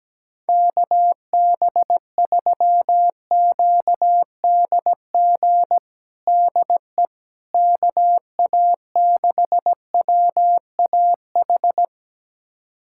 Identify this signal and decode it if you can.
Morse code, heard as a keyed tone, about 17 wpm, 700 Hz. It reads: KB3QDG DE KA6WAH
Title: USA Callsign DE Callsign - 6 Characters - 17wpm